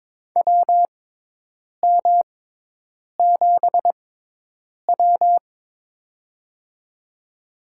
Morse code, heard as a keyed tone, about 22 wpm, spaced as for 8 wpm, 700 Hz. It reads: WM7W
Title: USA Call Signs - 6x Character Spacing - 22wpm